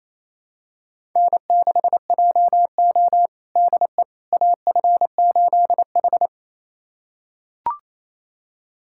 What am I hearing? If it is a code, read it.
Morse code, heard as a keyed tone, about 28 wpm, 700 Hz. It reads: N6JO DE AF8H
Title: USA Callsign DE Callsign - 4 Characters - 28wpm